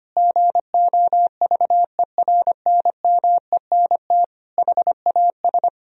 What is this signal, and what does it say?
Morse code, heard as a keyed tone, about 25 wpm, 700 Hz. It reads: GOVERNMENT HAS
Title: Sentences from Top 1000 Words - Review Part 3 - 25wpm